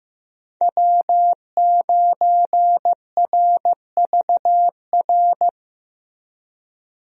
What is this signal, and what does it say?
Morse code, heard as a keyed tone, about 15 wpm, 700 Hz. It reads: W9RVR
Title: USA Call Signs - 5 Characters - 15wpm